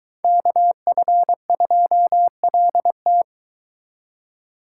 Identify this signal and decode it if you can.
Morse code, heard as a keyed tone, about 22 wpm, 700 Hz. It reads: KF2LT